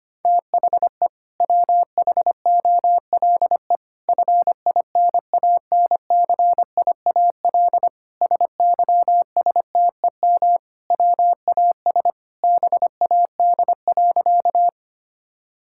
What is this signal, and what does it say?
Morse code, heard as a keyed tone, about 25 wpm, 700 Hz. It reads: THE WHOLE FINANCIAL SYSTEM WAS BAD.
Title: Sentences from Top 1000 Words - Review Part 5 - 25wpm